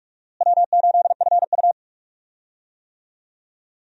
Morse code, heard as a keyed tone, about 42 wpm, 700 Hz. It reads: W8FU
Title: USA Call Signs - 4 Characters - Repeated 2x - 45wpm